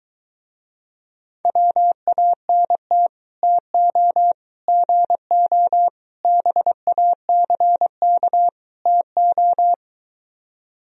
Morse code, heard as a keyed tone, about 23 wpm, 700 Hz. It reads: WANT TO GO BACK TO